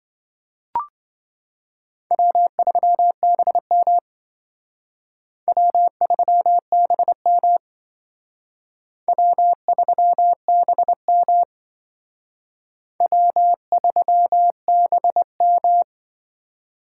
Morse code, heard as a keyed tone, about 26 wpm, 700 Hz. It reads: W3BM W3BM W3BM W3BM